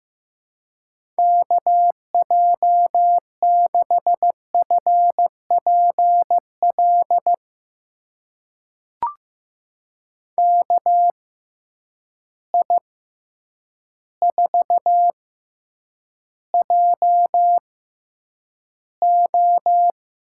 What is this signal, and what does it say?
Morse code, heard as a keyed tone, about 16 wpm, spaced as for 8 wpm, 700 Hz. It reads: KJ6FPL KI4JO
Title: USA Call Signs - 6 Characters - 6x Character Spacing - 15wpm